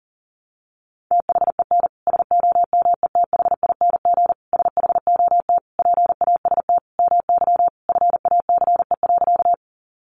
Morse code, heard as a keyed tone, about 40 wpm, 700 Hz. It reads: THEN SOMETHING SHOT PAST MY FACE.
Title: Sentences from Top 1100 Words 40wpm